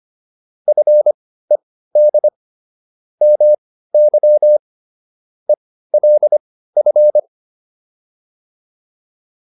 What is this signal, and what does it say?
Morse code, heard as a keyed tone, about 25 wpm, spaced as for 15 wpm, 600 Hz. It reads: FED MY ELF